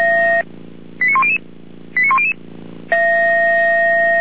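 Five-tone selective call (ZVEI), heard as quick runs of five tones, tones of about 70 ms. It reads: 89100 89100
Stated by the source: ZVEI five-tone selective call 89250 received by SDR at feuerwehr-orlamuende (fire brigade Orlamünde).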